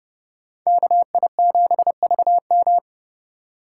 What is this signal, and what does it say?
Morse code, heard as a keyed tone, about 30 wpm, 700 Hz. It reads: KI7VM